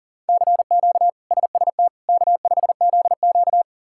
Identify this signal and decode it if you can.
Morse code, heard as a keyed tone, about 40 wpm, 700 Hz. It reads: CQ SST K5ZQ
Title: K1USN SST QSO Practice - 40wpm